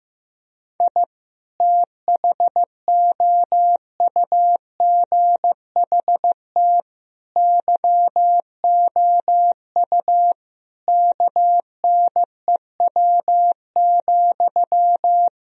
Morse code, heard as a keyed tone, about 15 wpm, 700 Hz. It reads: I THOUGHT YOU KNEW,
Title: Sentences from Top 1000 Words - Review Part 4 - 15wpm